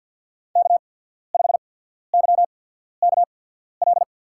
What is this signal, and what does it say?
Morse code, heard as a keyed tone, about 52 wpm, spaced as for 14 wpm, 700 Hz. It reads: K5YKL
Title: USA Call Signs - 5 Characters - 8x Character Spacing - 50wpm